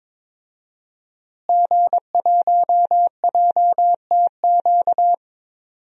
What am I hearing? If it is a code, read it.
Morse code, heard as a keyed tone, about 22 wpm, 700 Hz. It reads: G1JTQ
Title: DXCC-100 Call Signs - 5 Characters - Repeated 3x - 22wpm